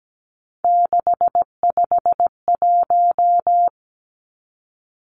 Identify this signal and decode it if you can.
Morse code, heard as a keyed tone, about 17 wpm, 700 Hz. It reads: 651
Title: Numbers - 3-Digit 17wpm